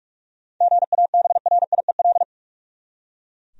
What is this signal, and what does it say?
Morse code, heard as a keyed tone, about 44 wpm, 700 Hz. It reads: GABRIEL
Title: USA Common Names - 45wpm